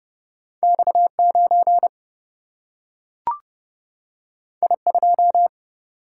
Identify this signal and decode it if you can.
Morse code, heard as a keyed tone, about 30 wpm, 700 Hz. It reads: X 9 I 2
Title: Single-Letter Single-Number 30wpm